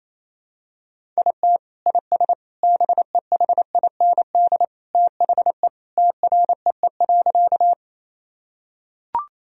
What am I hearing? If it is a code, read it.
Morse code, heard as a keyed tone, about 28 wpm, 700 Hz. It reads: IT IS BEHIND THE TREE.
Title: Sentences from Top 1000 Words - Review Part 1 - 28wpm